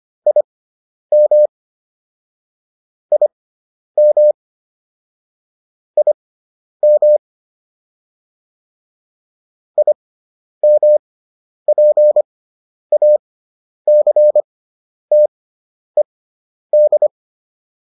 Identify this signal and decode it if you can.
Morse code, heard as a keyed tone, about 25 wpm, spaced as for 10 wpm, 600 Hz. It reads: IM IM IM IMPACTED